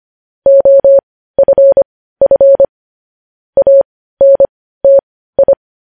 Morse code, heard as a keyed tone, about 25 wpm, spaced as for 15 wpm, 550 Hz. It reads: OFF ANTI
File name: podcast / News headlines-1x-25x15